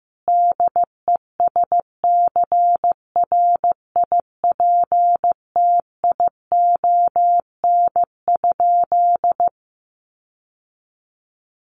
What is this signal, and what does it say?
Morse code, heard as a keyed tone, about 15 wpm, 700 Hz. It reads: DESCRIPTION?
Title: Sentences from Top 1300 Words 15wpm